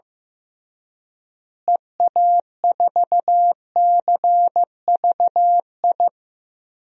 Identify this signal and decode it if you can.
Morse code, heard as a keyed tone, about 15 wpm, 700 Hz. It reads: EA4CVI
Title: DXCC-100 Call Signs - 6 Characters - Repeated 2x - 15wpm